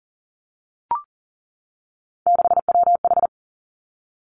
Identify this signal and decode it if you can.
Morse code, heard as a keyed tone, about 39 wpm, 690 Hz. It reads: 6WH
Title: Three Challenging Letter-Number Combinations 40wpm